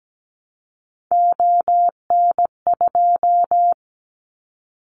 Morse code, heard as a keyed tone, about 17 wpm, 700 Hz. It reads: ON2
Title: Callsign Prefixes for Top 50 DXCC - 17wpm